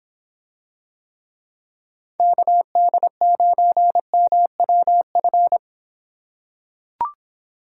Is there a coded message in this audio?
Yes: Morse code from 2.2 to 5.6 s, heard as a keyed tone, about 26 wpm, 700 Hz. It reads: KD9MWF